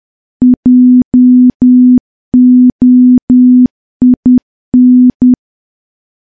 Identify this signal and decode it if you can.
Morse code, heard as a keyed tone, about 10 wpm, 250 Hz. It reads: JOIN
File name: morse / morse-training-10WPM